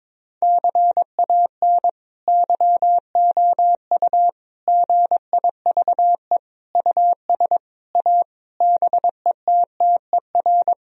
Morse code, heard as a keyed tone, about 21 wpm, 700 Hz. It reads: CAN YOU GIVE US A BETTER
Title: Sentences from Top 1000 Words - Review Part 4 - 22wpm